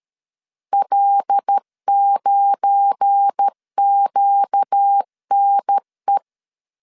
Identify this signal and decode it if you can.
Morse code, heard as a keyed tone, about 13 wpm, 800 Hz. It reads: L9QNE
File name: CW / DET / 1 9 1500C S C 12wpm 3811